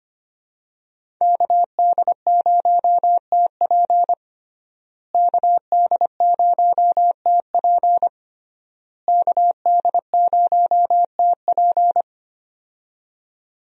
Morse code, heard as a keyed tone, about 25 wpm, 700 Hz. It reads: KD0TP KD0TP KD0TP
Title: USA Call Signs - 5 Characters - Repeated 3x - 25wpm